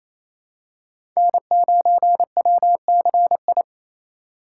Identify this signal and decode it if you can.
Morse code, heard as a keyed tone, about 28 wpm, 700 Hz. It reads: N9WCI
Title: USA Call Signs - 2x Character Spacing - 28wpm